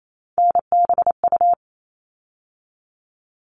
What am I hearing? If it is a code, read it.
Morse code, heard as a keyed tone, about 28 wpm, 700 Hz. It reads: NBU
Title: All Cut Numbers - 3-Digit 28wpm